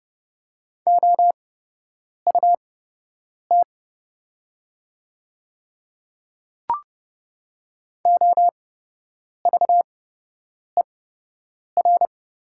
Morse code, heard as a keyed tone, about 30 wpm, spaced as for 8 wpm, 700 Hz. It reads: OUT OVER
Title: 8x Character Spacing - Top 100 Words - 30wpm